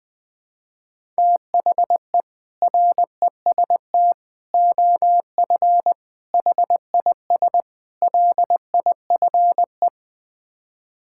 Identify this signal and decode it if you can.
Morse code, heard as a keyed tone, about 20 wpm, 700 Hz. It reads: THE REST OF HIS LIFE
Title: Build A Phrase - 5-Word Phrases - 20wpm